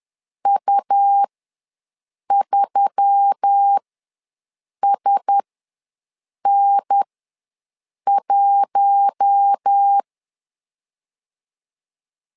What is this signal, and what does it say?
Morse code, heard as a keyed tone, about 10 wpm, spaced as for 6 wpm, 800 Hz. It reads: U3SN1
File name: LEZIONI / 4 Q1S3 ws 10wpm 2437